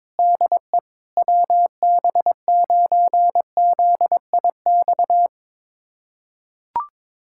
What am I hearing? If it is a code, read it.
Morse code, heard as a keyed tone, about 22 wpm, 700 Hz. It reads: DE WB9ZIX
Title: USA Callsign DE Callsign - 6 Characters - 22wpm